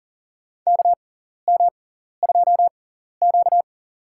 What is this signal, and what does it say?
Morse code, heard as a keyed tone, about 40 wpm, spaced as for 14 wpm, 700 Hz. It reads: KM2Q
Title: USA Call Signs - 6x Character Spacing - 40wpm